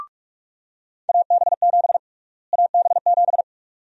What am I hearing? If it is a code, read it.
Morse code, heard as a keyed tone, about 42 wpm, 700 Hz. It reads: AB7 AB7
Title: USA Call Sign Prefixes - Repeated 2x - 45wpm